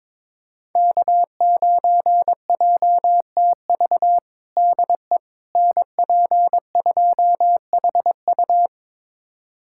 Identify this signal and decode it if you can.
Morse code, heard as a keyed tone, about 21 wpm, 700 Hz. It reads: K9JTV DE NP2HU
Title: USA Callsign DE Callsign - 5 Characters - 22wpm